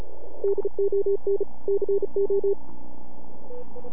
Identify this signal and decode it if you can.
Morse code, heard as a keyed tone, about 34 wpm, 400 Hz. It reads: DON CO
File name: NAQPCW-8-2009 / W1UJ NAQP CW 09 1930Z